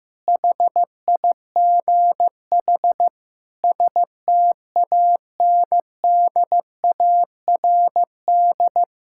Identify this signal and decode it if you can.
Morse code, heard as a keyed tone, about 14 wpm, 700 Hz. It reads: HIGH STANDARD
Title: Sentences from Top 1000 Words - Review Part 6 - 15wpm